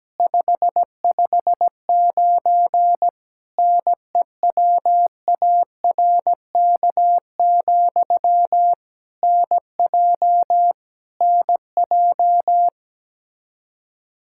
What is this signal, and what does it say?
Morse code, heard as a keyed tone, about 17 wpm, 700 Hz. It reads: 559 NEWARK, NJ NJ
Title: Practice QSO - Exchange 1 - Beginning - 17wpm